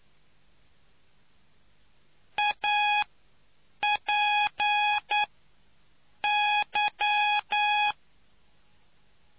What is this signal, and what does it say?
Morse code, heard as a keyed tone, about 9 wpm, spaced as for 6 wpm, 830 Hz. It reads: APY